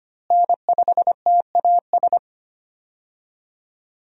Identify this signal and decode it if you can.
Morse code, heard as a keyed tone, about 25 wpm, 700 Hz. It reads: N5TAS
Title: USA Call Signs - 5 Characters - Repeated 3x - 25wpm